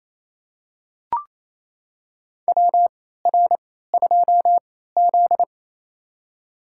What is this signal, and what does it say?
Morse code, heard as a keyed tone, about 27 wpm, spaced as for 16 wpm, 700 Hz. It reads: WR2Z
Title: USA Call Signs - 4 Characters - 3x Character Spacing - 28wpm